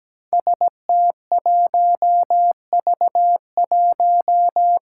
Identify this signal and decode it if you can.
Morse code, heard as a keyed tone, about 17 wpm, 700 Hz. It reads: ST1V1
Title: Five Challenging Letter-Number Combinations 17wpm